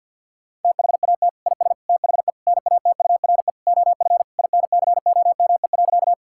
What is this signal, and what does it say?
Morse code, heard as a keyed tone, about 50 wpm, 700 Hz. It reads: THAT IS THE NATURE OF INCOME.
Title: Sentences from Top 1000 Words - Review Part 5 - 50wpm